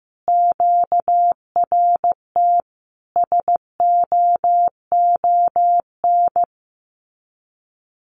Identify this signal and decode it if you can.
Morse code, heard as a keyed tone, about 15 wpm, 700 Hz. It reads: QRT SOON
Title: Common QSO Phrases 15wpm